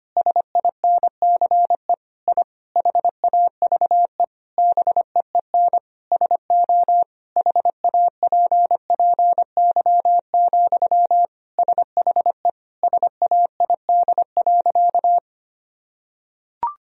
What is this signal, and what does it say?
Morse code, heard as a keyed tone, about 25 wpm, 700 Hz. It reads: SINCE I HAVE BEEN SO HAPPY, SHE SAID.
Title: Sentences from Top 1000 Words - Review Part 1 - 25wpm